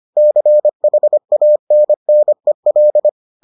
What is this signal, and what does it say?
Morse code, heard as a keyed tone, about 25 wpm, 600 Hz. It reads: CHANNEL